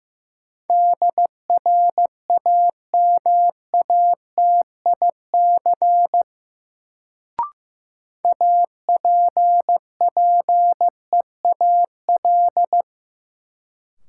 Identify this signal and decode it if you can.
Morse code, heard as a keyed tone, about 15 wpm, 700 Hz. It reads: DRAMATIC APPEAL
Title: Sentences from Top 1300 Words - New Word Drill - 15wpm